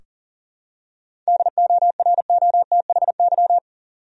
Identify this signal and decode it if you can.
Morse code, heard as a keyed tone, about 40 wpm, 700 Hz. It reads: DOROTHY
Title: USA Common Names - 40wpm